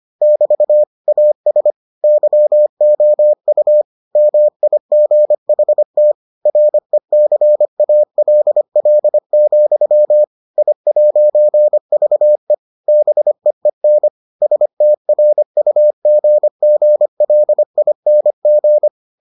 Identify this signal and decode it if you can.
Morse code, heard as a keyed tone, about 25 wpm, 600 Hz. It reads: <BT> AS YOU MIGHT RECALL, I'VE BEEN STRUGGLING